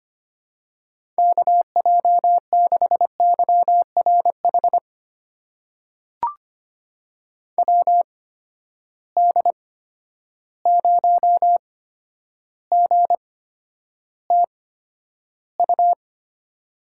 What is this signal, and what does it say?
Morse code, heard as a keyed tone, about 25 wpm, spaced as for 11 wpm, 700 Hz. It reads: KJ6YRH WD0GTU
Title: USA Call Signs - 6 Characters - 8x Character Spacing - 25wpm